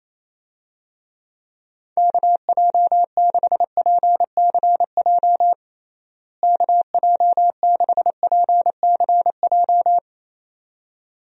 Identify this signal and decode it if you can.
Morse code, heard as a keyed tone, about 28 wpm, 700 Hz. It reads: KJ6PCJ KJ6PCJ
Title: USA Call Signs - 6 Characters - Repeated 2x - 28wpm